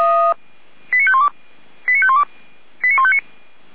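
Five-tone selective call (ZVEI), heard as quick runs of five tones, tones of about 70 ms. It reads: 89512 89512 89159
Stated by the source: ZVEI five-tone call 89160 received by SDR, from audio server of Feuerwehr Orlamünde (fire brigade).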